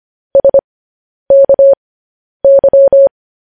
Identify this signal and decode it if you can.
Morse code, heard as a keyed tone, about 25 wpm, spaced as for 10 wpm, 550 Hz. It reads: SKY
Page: news-headlines-in-morse-code-at-10-wpm